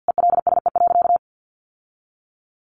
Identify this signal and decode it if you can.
Morse code, heard as a keyed tone, about 50 wpm, 700 Hz. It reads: ELSE.
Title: Sentences from Top 1100 Words 50wpm